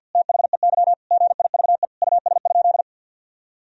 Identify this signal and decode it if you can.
Morse code, heard as a keyed tone, about 50 wpm, 700 Hz. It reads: THEY GIVE US?
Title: Sentences from Top 1000 Words - Review Part 2 - 50wpm